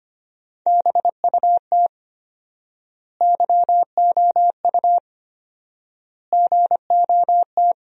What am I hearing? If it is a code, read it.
Morse code, heard as a keyed tone, about 25 wpm, 700 Hz. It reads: BUT YOU GOT
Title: 4x Word-Spacing - Sentences from Top 300 Words - 25wpm